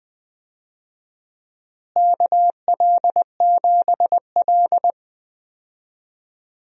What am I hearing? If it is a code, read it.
Morse code, heard as a keyed tone, about 20 wpm, 700 Hz. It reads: KL7L
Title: DXCC-100 Call Signs - 4 Characters - Repeated 3x - 20wpm